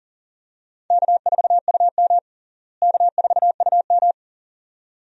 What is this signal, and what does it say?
Morse code, heard as a keyed tone, about 37 wpm, 700 Hz. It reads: K4UM K4UM
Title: USA Call Signs - 4 Characters - Repeated 2x - 40wpm